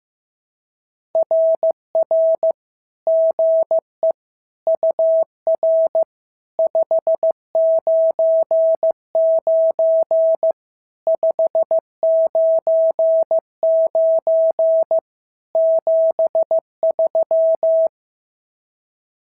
Morse code, heard as a keyed tone, about 14 wpm, 650 Hz. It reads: RR GE UR 599 599 73
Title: SOTA QSO Practice 15wpm